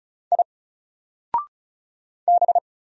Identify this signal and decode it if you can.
Morse code, heard as a keyed tone, about 36 wpm, 700 Hz. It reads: I B
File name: Single-Letter-Number-35wpm